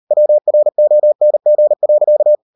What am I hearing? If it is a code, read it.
Morse code, heard as a keyed tone, about 38 wpm, 600 Hz. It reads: WRONG.